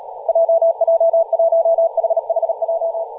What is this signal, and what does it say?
Morse code, heard as a keyed tone, about 38 wpm, 660 Hz, with radio noise. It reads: JJ1HHJ
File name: posts / 2016-aug-26-2236z-yj0cow-18-073mhz-cw